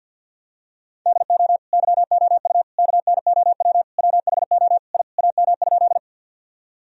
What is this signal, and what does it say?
Morse code, heard as a keyed tone, about 50 wpm, 700 Hz. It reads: DO YOU KNOW WHO I AM?
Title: Sentences from Top 1000 Words - Review Part 2 - 50wpm